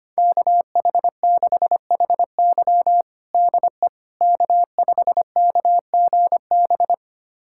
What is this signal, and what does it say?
Morse code, heard as a keyed tone, about 25 wpm, 700 Hz. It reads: KH6HY DE K5KGB
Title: USA Callsign DE Callsign - 5 Characters - 25wpm